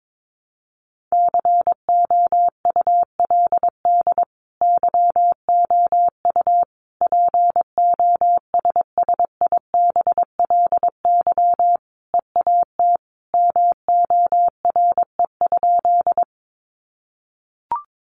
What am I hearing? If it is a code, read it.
Morse code, heard as a keyed tone, about 22 wpm, 700 Hz. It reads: COULD YOU POSSIBLY EAT MORE?
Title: Sentences from Top 1300 Words 22wpm